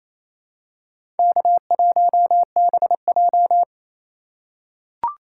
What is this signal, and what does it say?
Morse code, heard as a keyed tone, about 27 wpm, 700 Hz. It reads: K1BJ
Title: USA Call Signs - 4 Characters - 6x Character Spacing - 28wpm